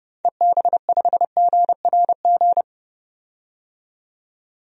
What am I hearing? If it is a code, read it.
Morse code, heard as a keyed tone, about 30 wpm, 700 Hz. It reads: EB5GRG
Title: DXCC-100 Call Signs - 6 Characters - 30wpm